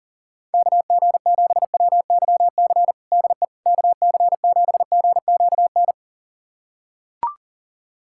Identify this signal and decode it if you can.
Morse code, heard as a keyed tone, about 40 wpm, 700 Hz. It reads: KG7WYC DE KC7GQN
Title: USA Callsign DE Callsign 40wpm